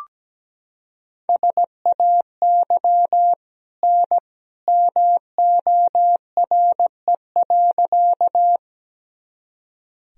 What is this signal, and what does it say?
Morse code, heard as a keyed tone, about 17 wpm, 700 Hz. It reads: SAY N MORE.
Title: Sentences from Top 200 Words and Abbreviations 17wpm